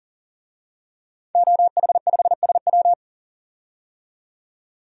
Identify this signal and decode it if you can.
Morse code, heard as a keyed tone, about 40 wpm, 700 Hz. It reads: OH5SW
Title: DXCC-100 Call Signs - 5 Characters - 40wpm